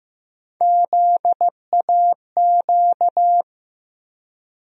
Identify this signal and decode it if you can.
Morse code, heard as a keyed tone, about 15 wpm, 700 Hz. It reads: ZAQ
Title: USA Call Sign Postfixes - 3 Letters 15wpm